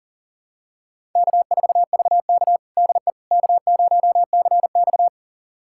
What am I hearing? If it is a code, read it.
Morse code, heard as a keyed tone, about 40 wpm, 700 Hz. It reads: K4VK DE K0CX
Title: USA Callsign DE Callsign - 4 Characters - 40wpm